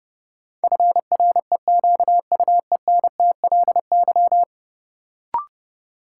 Morse code, heard as a keyed tone, about 30 wpm, 700 Hz. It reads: FREQUENTLY
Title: Sentences from Top 1300 Words - New Word Drill Alt - 30wpm